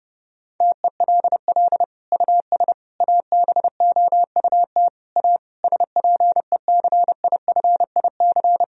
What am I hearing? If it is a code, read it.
Morse code, heard as a keyed tone, about 30 wpm, 700 Hz. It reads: TELL US ABOUT A SPECIFIC
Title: Sentences from Top 1000 Words - Review Part 3 - 30wpm